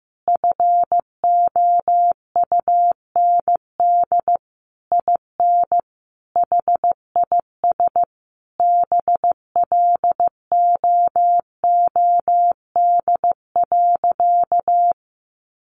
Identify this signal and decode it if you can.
Morse code, heard as a keyed tone, about 15 wpm, 700 Hz. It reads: FOUND IN HIS BLOOD.
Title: Sentences from Top 1400 Words 15wpm